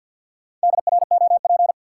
Morse code, heard as a keyed tone, about 52 wpm, 700 Hz. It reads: DROP
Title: Sentences from Top 1400 Words - New Word Drill - 50wpm